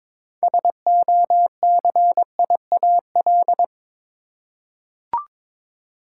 Morse code, heard as a keyed tone, about 22 wpm, 700 Hz. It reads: SOCIAL
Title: Practice QSO - Exchange 3 - More Info - 22wpm